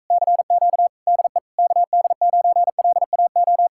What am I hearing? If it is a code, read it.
Morse code, heard as a keyed tone, about 44 wpm, 700 Hz. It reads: CQ DE KD9LAO